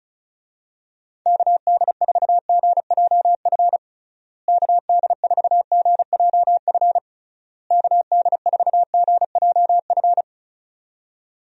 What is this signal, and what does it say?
Morse code, heard as a keyed tone, about 34 wpm, 700 Hz. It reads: KD4GJF KD4GJF KD4GJF
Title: USA Call Signs - 6 Characters - Repeated 3x - 35wpm